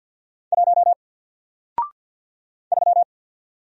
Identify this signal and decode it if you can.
Morse code, heard as a keyed tone, about 50 wpm, 700 Hz. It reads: 1 3
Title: Single Letter-Number Alt 50wpm